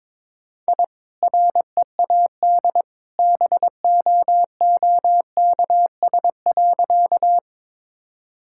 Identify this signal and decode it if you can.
Morse code, heard as a keyed tone, about 22 wpm, 700 Hz. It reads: I READ BOOKS.
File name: practice / 22wpm / Easy Sentences with Hints - 22wpm